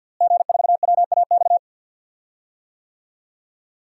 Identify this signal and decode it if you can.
Morse code, heard as a keyed tone, about 51 wpm, 700 Hz. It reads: G4WAX
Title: DXCC-100 Call Signs - 5 Characters - Repeated 3x - 50wpm